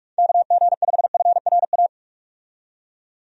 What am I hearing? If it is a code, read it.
Morse code, heard as a keyed tone, about 46 wpm, 700 Hz. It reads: KG5FRA
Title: USA Call Signs - 6 Characters - 45wpm